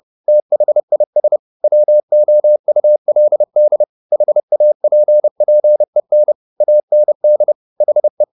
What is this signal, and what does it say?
Morse code, heard as a keyed tone, about 30 wpm, 600 Hz. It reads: THIS WOULD HAPPEN AND HE